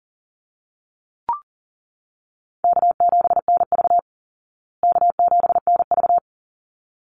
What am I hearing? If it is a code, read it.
Morse code, heard as a keyed tone, about 40 wpm, 700 Hz. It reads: K7NV K7NV
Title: Easy Call Signs Repeated 2x - 40wpm